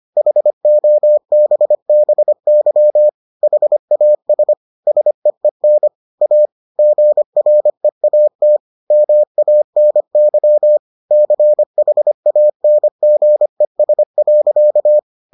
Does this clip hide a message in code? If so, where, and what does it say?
Yes: Morse code from 0.2 to 15.0 s, heard as a keyed tone, about 25 wpm, 600 Hz. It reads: HOBBY HAS SEEN A GREAT MANY CHANGES.